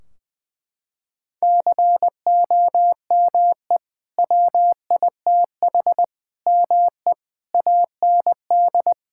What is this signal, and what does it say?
Morse code, heard as a keyed tone, about 20 wpm, 700 Hz. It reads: COME WITH ME AND
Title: Sentences from Top 1600 Words 20wpm